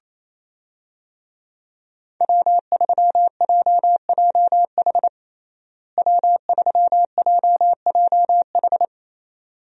Morse code, heard as a keyed tone, about 27 wpm, 700 Hz. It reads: W3JJH W3JJH
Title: USA Call Signs - 5 Characters - Repeated 2x - 28wpm